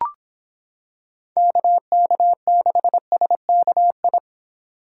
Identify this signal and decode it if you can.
Morse code, heard as a keyed tone, about 25 wpm, 700 Hz. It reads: KK6SKI